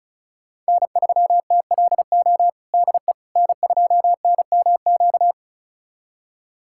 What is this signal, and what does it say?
Morse code, heard as a keyed tone, about 34 wpm, 700 Hz. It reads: N3TLO DE N2NMQ
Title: USA Callsign DE Callsign - 5 Characters - 35wpm